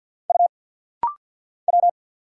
Morse code, heard as a keyed tone, about 50 wpm, 700 Hz. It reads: U W
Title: Single Letter-Number - Rapid-Fire 50wpm